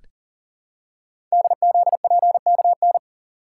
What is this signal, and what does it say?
Morse code, heard as a keyed tone, about 39 wpm, 690 Hz. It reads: DZPKN
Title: Code Groups - 5 Letters - 40wpm